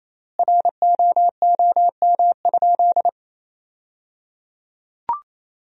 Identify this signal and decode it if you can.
Morse code, heard as a keyed tone, about 28 wpm, 700 Hz. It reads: ROOM?
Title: 3x Word-Spacing - Sentences from Top 300 Words - 28wpm